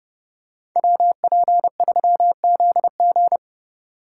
Morse code, heard as a keyed tone, about 30 wpm, 700 Hz. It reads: WP3ZG